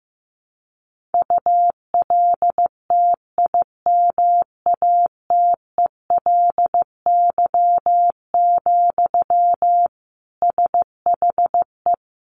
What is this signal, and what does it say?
Morse code, heard as a keyed tone, about 15 wpm, 700 Hz. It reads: ULTIMATELY, SHE
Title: Sentences from Top 1200 Words 15wpm